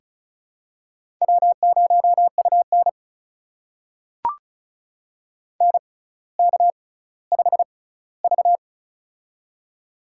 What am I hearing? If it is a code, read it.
Morse code, heard as a keyed tone, about 36 wpm, spaced as for 18 wpm, 700 Hz. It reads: W0UN NK5V
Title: USA Call Signs - 4 Characters - 6x Character Spacing - 35wpm